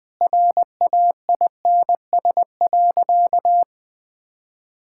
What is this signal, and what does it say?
Morse code, heard as a keyed tone, about 21 wpm, 700 Hz. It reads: RAINS.
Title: Sentences with 5-Word Phrases - Spoken Beginning - 20wpm